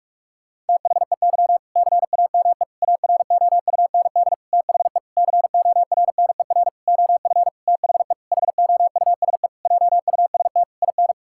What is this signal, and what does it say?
Morse code, heard as a keyed tone, about 46 wpm, 700 Hz. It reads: THEY CAME AROUND THE CORNER OF THE HOUSE JUST IN
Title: Sentences from Top 1600 Words 45wpm